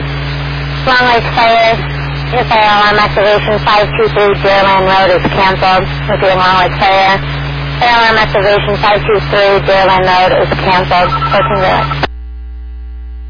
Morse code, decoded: none

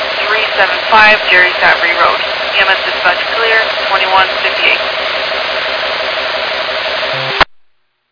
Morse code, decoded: none

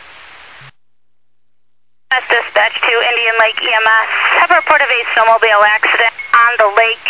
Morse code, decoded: none

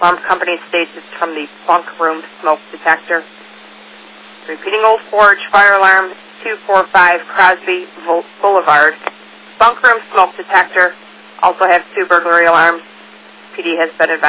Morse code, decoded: none

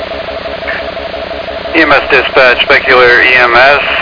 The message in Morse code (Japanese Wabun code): none

